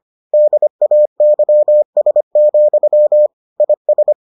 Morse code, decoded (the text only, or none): DAYS, IS